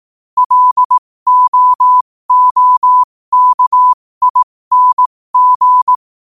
Morse code, heard as a keyed tone, about 18 wpm, spaced as for 15 wpm, 1000 Hz. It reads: LOOKING
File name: qotdfiles / 15wpm